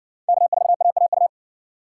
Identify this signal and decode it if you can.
Morse code, heard as a keyed tone, about 56 wpm, 700 Hz. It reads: K4AAU